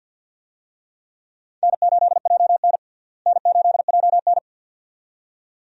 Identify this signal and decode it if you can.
Morse code, heard as a keyed tone, about 52 wpm, 700 Hz. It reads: N8JN N8JN
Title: USA Call Signs - 4 Characters - Repeated 2x - 50wpm